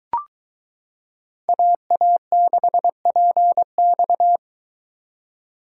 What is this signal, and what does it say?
Morse code, heard as a keyed tone, about 22 wpm, 700 Hz. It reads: AA6PX